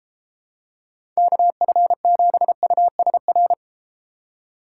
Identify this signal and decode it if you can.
Morse code, heard as a keyed tone, about 29 wpm, 700 Hz. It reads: KF7USR